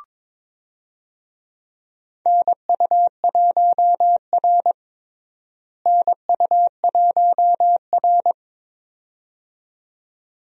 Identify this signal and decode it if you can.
Morse code, heard as a keyed tone, about 22 wpm, 700 Hz. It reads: NU1R NU1R
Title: USA Call Signs - 4 Characters - Repeated 2x - 22wpm